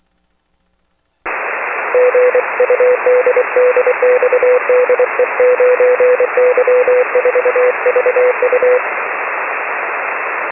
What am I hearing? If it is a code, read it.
Morse code, heard as a keyed tone, about 24 wpm, 500 Hz, with radio noise. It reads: GUD DX DE 9Y4VU
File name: 6m-9y4vu